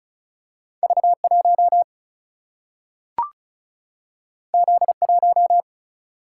Morse code, heard as a keyed tone, about 36 wpm, 700 Hz. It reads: V 1 Z 1
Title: Single-Letter Single-Number 35wpm